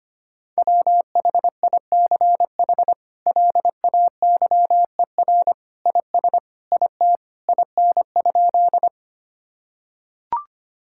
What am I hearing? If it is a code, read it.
Morse code, heard as a keyed tone, about 25 wpm, 700 Hz. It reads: WHICH LAYER IS IT IN?
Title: Sentences from Top 1600 Words 25wpm